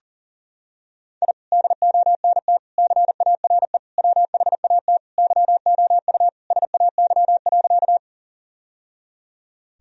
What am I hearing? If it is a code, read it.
Morse code, heard as a keyed tone, about 37 wpm, 700 Hz. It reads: I DONT CARE WHAT YOU SAY.
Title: Sentences from Top 1000 Words - Review Part 1 - 40wpm